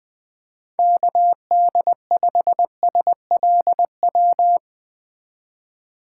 Morse code, heard as a keyed tone, about 21 wpm, 700 Hz. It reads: KD5SLW
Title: USA Call Signs - 6 Characters - 2x Character Spacing - 20wpm